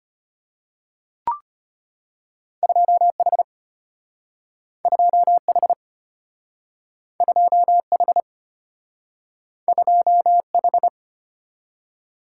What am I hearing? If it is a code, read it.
Morse code, heard as a keyed tone, about 34 wpm, 700 Hz. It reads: 2H 2H 2H 2H